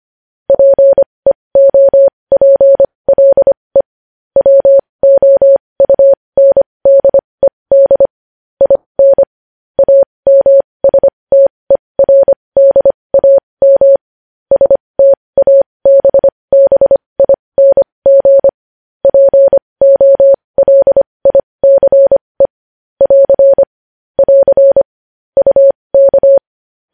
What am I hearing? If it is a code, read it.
Morse code, heard as a keyed tone, about 25 wpm, spaced as for 20 wpm, 550 Hz. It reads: PEOPLE WOUNDED IN AMSTERDAM STABBING POLICE <AR> <AR> UK
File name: podcast / News headlines-1x-25x20